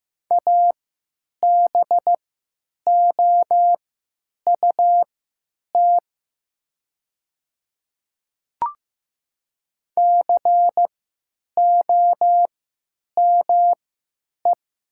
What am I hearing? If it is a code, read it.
Morse code, heard as a keyed tone, about 15 wpm, spaced as for 9 wpm, 700 Hz. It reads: ABOUT COME
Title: 3x Character Spacing - Top 300 Words - 15wpm